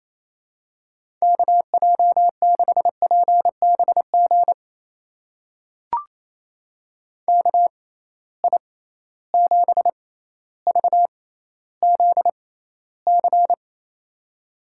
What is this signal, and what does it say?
Morse code, heard as a keyed tone, about 28 wpm, spaced as for 14 wpm, 700 Hz. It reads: KJ6PBG KI7VZC
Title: USA Call Signs - 6 Characters - 6x Character Spacing - 28wpm